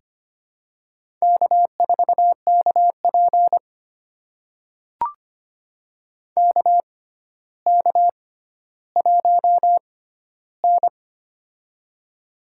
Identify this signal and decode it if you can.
Morse code, heard as a keyed tone, about 25 wpm, spaced as for 13 wpm, 700 Hz. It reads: K4KP KK1N
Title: USA Call Signs - 4 Characters - 6x Character Spacing - 25wpm